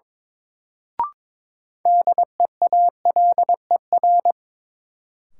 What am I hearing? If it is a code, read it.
Morse code, heard as a keyed tone, about 23 wpm, 700 Hz. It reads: DEALER
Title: Sentences from Top 1100 Words - New Word Drill - 22wpm